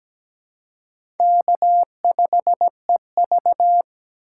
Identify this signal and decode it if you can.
Morse code, heard as a keyed tone, about 17 wpm, 700 Hz. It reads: K5EV
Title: USA Call Signs - 4 Characters - 2x Character Spacing - 17wpm